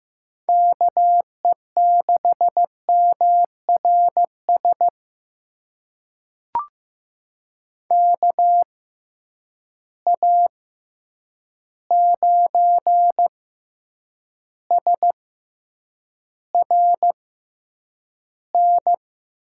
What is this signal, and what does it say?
Morse code, heard as a keyed tone, about 15 wpm, spaced as for 8 wpm, 700 Hz. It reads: KE6MRS KA9SRN
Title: USA Call Signs - 6 Characters - 6x Character Spacing - 15wpm